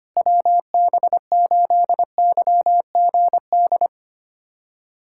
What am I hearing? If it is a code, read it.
Morse code, heard as a keyed tone, about 25 wpm, 700 Hz. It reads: WB8YGD
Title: USA Callsign DE Callsign 25wpm